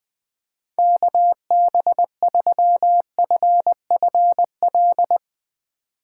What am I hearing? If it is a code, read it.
Morse code, heard as a keyed tone, about 20 wpm, 700 Hz. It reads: KB3FFL
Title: Speed Racing - USA 2x3 Call Signs 20wpm